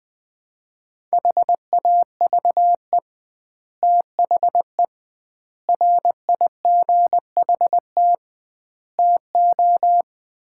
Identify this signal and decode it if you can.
Morse code, heard as a keyed tone, about 21 wpm, 700 Hz. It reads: HAVE THE RIGHT TO
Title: 2x Word-Spacing - Sets of 4 Words - 20wpm